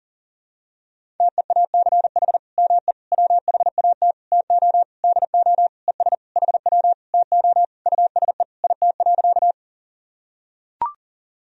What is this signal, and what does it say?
Morse code, heard as a keyed tone, about 37 wpm, 700 Hz. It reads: TEACH ME WHAT TO DO ES HW TO USE IT.
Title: Sentences from Top 500 Words and Abbreviations 40wpm